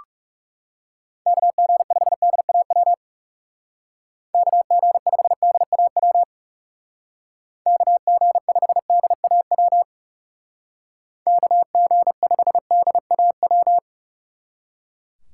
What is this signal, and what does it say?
Morse code, heard as a keyed tone, about 37 wpm, 700 Hz. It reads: KG5DAW KG5DAW KG5DAW KG5DAW
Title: Speed Racing - USA 2x3 Call Signs 30wpm